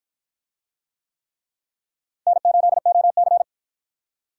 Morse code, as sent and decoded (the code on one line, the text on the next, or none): -. ---.. --- -.-.
N8OC